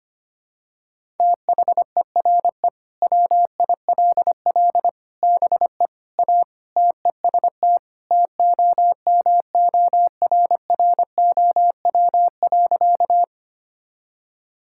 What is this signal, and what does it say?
Morse code, heard as a keyed tone, about 25 wpm, 700 Hz. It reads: THERE WILL BE A TEST TOMORROW.
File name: practice / 25wpm / Sentences with 3-Word Phrases - 25wpm